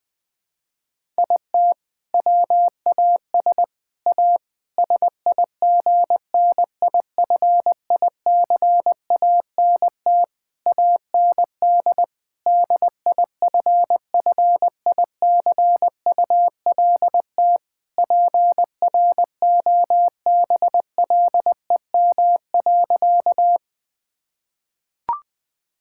Morse code, heard as a keyed tone, about 21 wpm, 700 Hz. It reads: IT WAS A SIGNIFICANT AND DIFFICULT PROBLEM.
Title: Sentences from Top 1000 Words - Review Part 6 - 20wpm